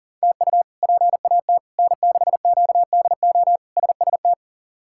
Morse code, heard as a keyed tone, about 37 wpm, 700 Hz. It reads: TU PAT N6QDO SST
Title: K1USN SST QSO Practice - 40wpm